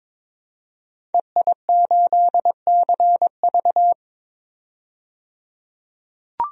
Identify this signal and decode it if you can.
Morse code, heard as a keyed tone, about 21 wpm, 700 Hz. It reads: EI8CV
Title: DXCC-100 Call Signs - 5 Characters - 22wpm